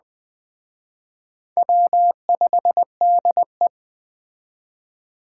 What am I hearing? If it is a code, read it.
Morse code, heard as a keyed tone, about 21 wpm, 700 Hz. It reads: W5DE